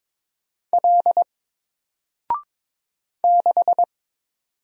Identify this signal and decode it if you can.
Morse code, heard as a keyed tone, about 22 wpm, 700 Hz. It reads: L 6
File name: Single-Letter-Number-22wpm